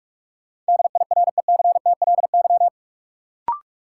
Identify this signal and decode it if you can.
Morse code, heard as a keyed tone, about 46 wpm, 700 Hz. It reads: DIRECTLY
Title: Sentences from Top 800 Words - New Word Drill - 45wpm